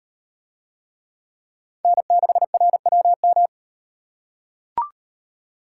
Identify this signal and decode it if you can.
Morse code, heard as a keyed tone, about 38 wpm, 700 Hz. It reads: N6RWM